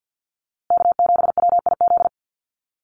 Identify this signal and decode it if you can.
Morse code, heard as a keyed tone, about 50 wpm, 700 Hz. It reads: K7WIZ
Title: USA Call Signs Repeated 2x 50wpm